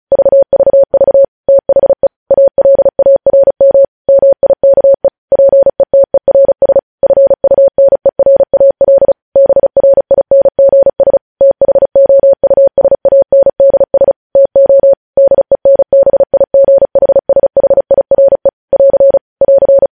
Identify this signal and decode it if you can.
Morse code, heard as a keyed tone, about 36 wpm, 550 Hz. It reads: VVV THE ALARM MIKE PETERS FUNERAL BRINGS THOUSANDS TO DENBIGHSHIRE <AR> <AR>
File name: podcast / News headlines-1x-35x35